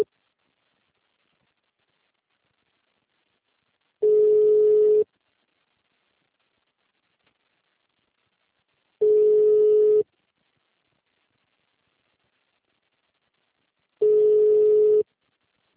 Ringback tone (Europe, 425 Hz).